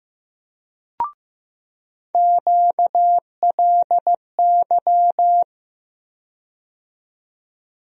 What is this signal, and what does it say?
Morse code, heard as a keyed tone, about 15 wpm, 700 Hz. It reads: QLY
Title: USA Call Sign Postfixes - 3 Letters 15wpm